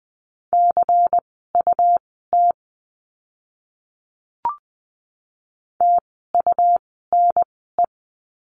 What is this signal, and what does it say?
Morse code, heard as a keyed tone, about 20 wpm, spaced as for 15 wpm, 700 Hz. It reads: CUT TUNE
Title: 101 Lesson 17f - Letter u - Words 20wpm